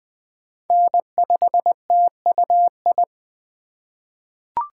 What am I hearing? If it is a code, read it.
Morse code, heard as a keyed tone, about 20 wpm, 700 Hz. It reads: N5TUI
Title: USA Call Signs - 2x Character Spacing - 20wpm